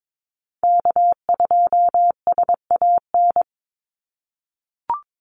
Morse code, heard as a keyed tone, about 22 wpm, 700 Hz. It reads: K2SAN
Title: Call Signs - 22wpm